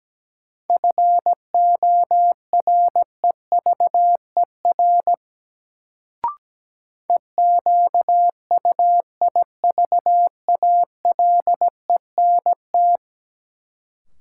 Morse code, heard as a keyed tone, about 17 wpm, 700 Hz. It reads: FOREVER EQUIVALENT
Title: Sentences from Top 1600 Words - New Word Drill Alt - 17wpm